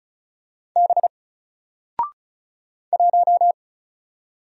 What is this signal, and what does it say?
Morse code, heard as a keyed tone, about 35 wpm, 700 Hz. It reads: B 1
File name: Single-Letter-Number-35wpm